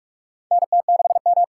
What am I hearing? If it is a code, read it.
Morse code, heard as a keyed tone, about 46 wpm, 700 Hz. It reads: NT6M
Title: USA Call Signs - 4 Characters - 4x Character Spacing - 45wpm